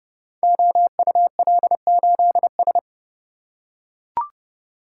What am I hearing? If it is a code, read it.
Morse code, heard as a keyed tone, about 32 wpm, 700 Hz. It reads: OUL8S
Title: Code Groups - 5 Letter:Numbers - 30wpm